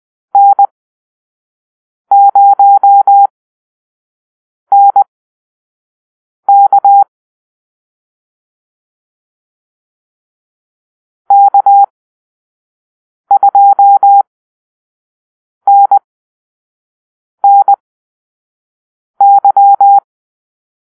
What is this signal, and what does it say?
Morse code, heard as a keyed tone, about 20 wpm, spaced as for 5 wpm, 800 Hz. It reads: N0NK K2NNY
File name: arrl / Call-Sign Drill-4